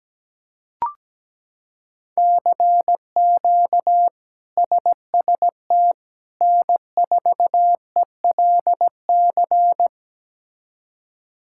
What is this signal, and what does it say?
Morse code, heard as a keyed tone, about 17 wpm, 700 Hz. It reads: CQ SST N4ELC
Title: K1USN SST QSO Practice - 17wpm